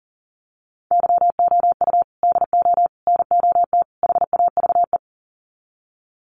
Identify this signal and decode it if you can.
Morse code, heard as a keyed tone, about 40 wpm, 700 Hz. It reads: YOU DO NOT HAVE
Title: Sets of 4 Words 40wpm